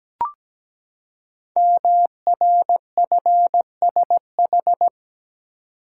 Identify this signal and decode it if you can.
Morse code, heard as a keyed tone, about 17 wpm, 700 Hz. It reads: MRFSH
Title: Code Groups - 5 Letters - 17wpm